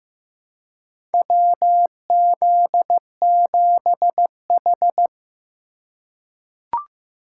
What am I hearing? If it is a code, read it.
Morse code, heard as a keyed tone, about 15 wpm, 700 Hz. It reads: WZ7H
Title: USA Call Signs - 4 Characters - 2x Character Spacing - 15wpm